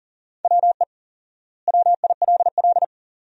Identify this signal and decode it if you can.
Morse code, heard as a keyed tone, about 40 wpm, 700 Hz. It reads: WE WILL
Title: 4x Word-Spacing - Sentences from Top 400 Words - 40wpm